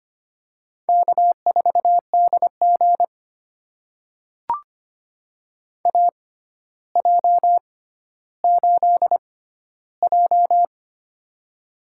Morse code, heard as a keyed tone, about 25 wpm, spaced as for 13 wpm, 700 Hz. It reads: K4DG AJ8J
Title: USA Call Signs - 4 Characters - 6x Character Spacing - 25wpm